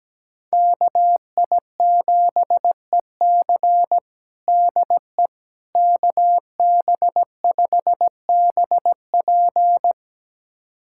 Morse code, heard as a keyed tone, about 17 wpm, 700 Hz. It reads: KI7EC DE KB5BP
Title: USA Callsign DE Callsign - 5 Characters - 17wpm